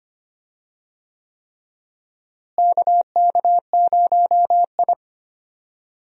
Morse code, heard as a keyed tone, about 25 wpm, 700 Hz. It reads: KK0I
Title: USA Call Signs - 4 Characters - Repeated 3x - 25wpm